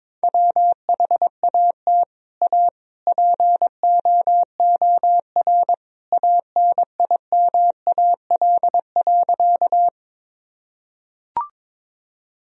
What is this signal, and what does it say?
Morse code, heard as a keyed tone, about 22 wpm, 700 Hz. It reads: WHAT A POOR ANIMAL.
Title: Sentences from Top 1000 Words - Review Part 1 - 22wpm